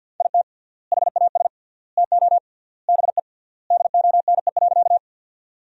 Morse code, heard as a keyed tone, about 52 wpm, 700 Hz. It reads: IT HAS TO BE DONE.
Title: 3x Word-Spacing - Sentences from Top 500 Words - 50wpm